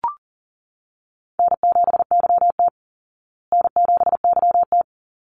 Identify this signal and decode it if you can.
Morse code, heard as a keyed tone, about 40 wpm, 700 Hz. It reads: N7YT N7YT
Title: Easy Call Signs Repeated 2x - 40wpm